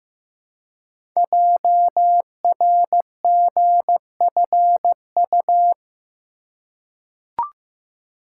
Morse code, heard as a keyed tone, about 15 wpm, 700 Hz. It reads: JRGFU
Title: Code Groups - 5 Letters - 15wpm